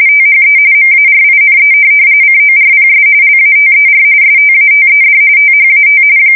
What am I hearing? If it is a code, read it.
RTTY, about 45.45 baud. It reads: THE MARINE CORPS LATER STATED THAT
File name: fsk170 / USS-New-Jersey